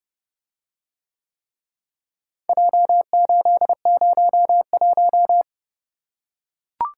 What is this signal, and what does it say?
Morse code, heard as a keyed tone, about 30 wpm, 700 Hz. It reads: J801